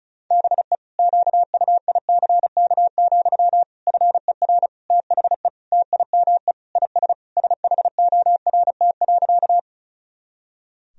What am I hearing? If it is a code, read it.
Morse code, heard as a keyed tone, about 36 wpm, 700 Hz. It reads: BE QUICK, FER THE TIME IS SHORT.
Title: Sentences from Top 500 Words and Abbreviations 35wpm